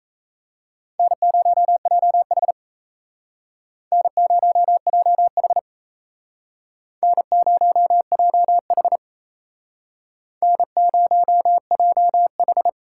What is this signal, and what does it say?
Morse code, heard as a keyed tone, about 37 wpm, 700 Hz. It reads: N0JH N0JH N0JH N0JH